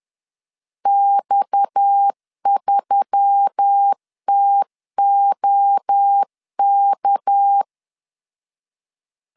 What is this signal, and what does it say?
Morse code, heard as a keyed tone, about 10 wpm, 800 Hz. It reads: X3TOK
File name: LEZIONI / 6 PKX0 / 6 PKX 10wpm 2426